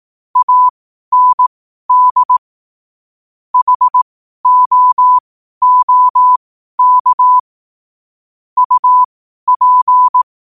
Morse code, heard as a keyed tone, about 18 wpm, spaced as for 12 wpm, 1000 Hz. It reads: AND HOOK UP
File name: qotdfiles / 12wpm